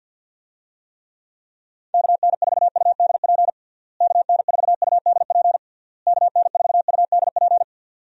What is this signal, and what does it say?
Morse code, heard as a keyed tone, about 51 wpm, 700 Hz. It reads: KN4UDP KN4UDP KN4UDP
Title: USA Call Signs - 6 Characters - Repeated 3x - 50wpm